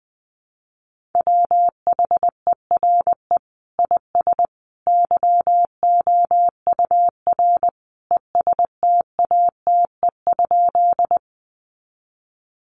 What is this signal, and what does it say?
Morse code, heard as a keyed tone, about 20 wpm, 700 Hz. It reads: WHERE IS YOUR ESTATE?
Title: Sentences from Top 1200 Words 20wpm